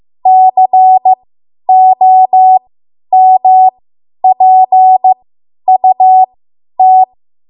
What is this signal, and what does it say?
Morse code, heard as a keyed tone, about 15 wpm, spaced as for 10 wpm, 750 Hz. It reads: COMPUT
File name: Morse / 10WPM